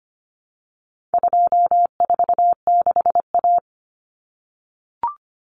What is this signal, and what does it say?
Morse code, heard as a keyed tone, about 25 wpm, 700 Hz. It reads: 246A
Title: Common Cut Numbers - 4-Digit 25wpm